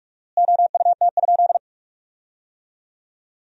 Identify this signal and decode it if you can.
Morse code, heard as a keyed tone, about 42 wpm, 700 Hz. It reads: OUT?
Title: 4x Word-Spacing - Sentences from Top 200 Words - 45wpm